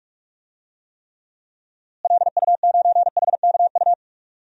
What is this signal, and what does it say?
Morse code, heard as a keyed tone, about 47 wpm, 700 Hz. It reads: LU9HKU
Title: DXCC-100 Call Signs - 6 Characters - Repeated 3x - 45wpm